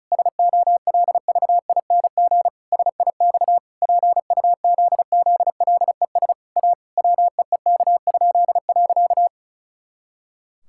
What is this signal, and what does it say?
Morse code, heard as a keyed tone, about 36 wpm, 700 Hz. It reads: SOLVING SIX PUZZLES A WEEK?.
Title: Pangram Challenge v2 - 35wpm